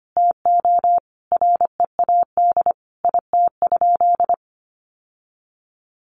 Morse code, heard as a keyed tone, about 25 wpm, 700 Hz. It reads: TO READ IT?
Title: Sentences from Top 1200 Words 25wpm